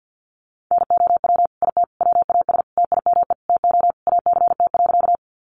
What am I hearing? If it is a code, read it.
Morse code, heard as a keyed tone, about 48 wpm, 700 Hz. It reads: NOW IT WAS TIME TO ACT.
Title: Sentences from Top 200 Words 50wpm